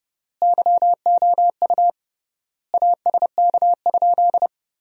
Morse code, heard as a keyed tone, about 32 wpm, 700 Hz. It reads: YOU ASK?
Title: 3x Word-Spacing - Sentences from Top 200 Words - 30wpm